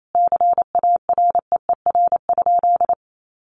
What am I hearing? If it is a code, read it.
Morse code, heard as a keyed tone, about 28 wpm, 700 Hz. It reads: CAREER?
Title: Sentences from Top 1000 Words 28wpm